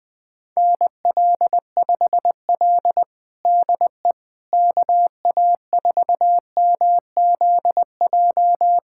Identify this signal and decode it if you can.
Morse code, heard as a keyed tone, about 21 wpm, 700 Hz. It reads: NL5L DE KA4MZJ
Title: USA Callsign DE Callsign 20wpm